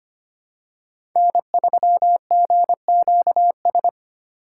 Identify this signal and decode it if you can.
Morse code, heard as a keyed tone, about 25 wpm, 700 Hz. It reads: N3GQS
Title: USA Call Signs - 5 Characters - 3x Character Spacing - 25wpm